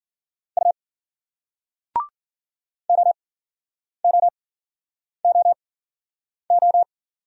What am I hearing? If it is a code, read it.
Morse code, heard as a keyed tone, about 51 wpm, spaced as for 9 wpm, 700 Hz. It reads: U OOOO